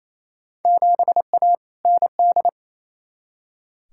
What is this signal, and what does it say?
Morse code, heard as a keyed tone, about 28 wpm, 700 Hz. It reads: 7A ND
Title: Field Day Exchanges Encore - 28wpm